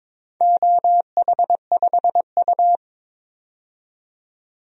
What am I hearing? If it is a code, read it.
Morse code, heard as a keyed tone, about 22 wpm, 700 Hz. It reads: OH5U
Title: DXCC-100 Call Signs - 4 Characters - 22wpm